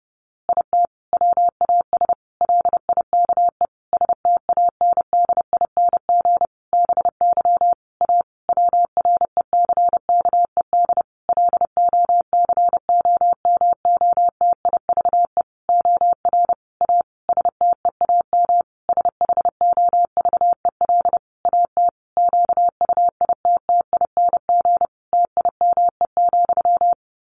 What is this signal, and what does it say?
Morse code, heard as a keyed tone, about 30 wpm, 700 Hz. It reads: IT WAS LIKE STANDING BY A WRECKED LOCOMOTIVE OR A STEAM SHOVEL AT QUITTING TIME,
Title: A Sound Of Thunder 30wpm